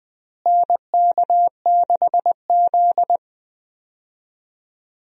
Morse code, heard as a keyed tone, about 19 wpm, 700 Hz. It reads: NK6Z
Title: USA Call Signs - 4 Characters - Repeated 3x - 20wpm